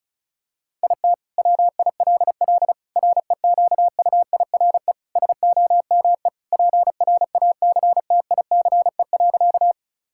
Morse code, heard as a keyed tone, about 35 wpm, 700 Hz. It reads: IT WILL REQUIRE SOME PRACTICE.
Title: Sentences from Top 1000 Words - Review Part 6 - 35wpm